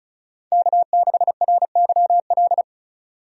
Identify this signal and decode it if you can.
Morse code, heard as a keyed tone, about 36 wpm, 700 Hz. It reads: K6RYL
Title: USA Call Signs - 5 Characters - 35wpm